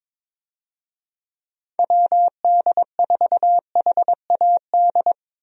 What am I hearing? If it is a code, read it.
Morse code, heard as a keyed tone, about 22 wpm, 700 Hz. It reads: WD4HAD
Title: USA Call Signs - 6 Characters - Repeated 3x - 22wpm